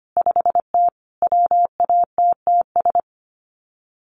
Morse code, heard as a keyed tone, about 25 wpm, 700 Hz. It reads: 5T WATTS
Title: Common QSO Phrases 25wpm